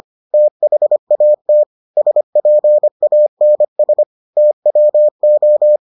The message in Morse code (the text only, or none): THAT SPANS TWO